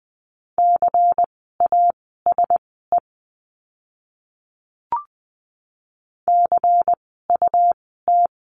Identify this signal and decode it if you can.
Morse code, heard as a keyed tone, about 20 wpm, spaced as for 15 wpm, 700 Hz. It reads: CASE CUT
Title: 101 Lesson 17f - Letter u - Words 20wpm